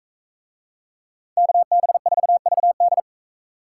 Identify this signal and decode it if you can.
Morse code, heard as a keyed tone, about 41 wpm, 700 Hz. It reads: KB4VD